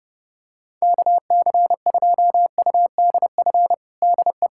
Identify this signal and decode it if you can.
Morse code, heard as a keyed tone, about 30 wpm, 700 Hz. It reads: KC2UDF DE
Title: USA Callsign DE Callsign - 6 Characters - 30wpm